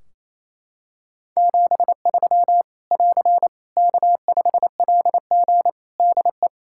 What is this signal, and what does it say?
Morse code, heard as a keyed tone, about 28 wpm, 700 Hz. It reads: 73 <AR> K5LG DE
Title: K1USN SST QSO Practice 28wpm